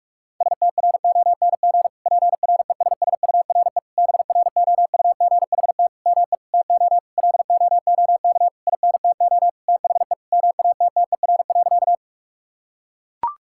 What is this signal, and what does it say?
Morse code, heard as a keyed tone, about 41 wpm, 700 Hz. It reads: STRONG PRESSURE BROUGHT ME TO LOOK INTO THE MATTER.
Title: Sentences from Top 1000 Words - Review Part 6 - 45wpm